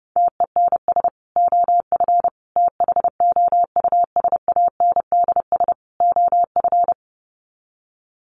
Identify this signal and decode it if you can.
Morse code, heard as a keyed tone, about 30 wpm, 700 Hz. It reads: TENS OF THOUSANDS OF
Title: Sets of 4 Words 30wpm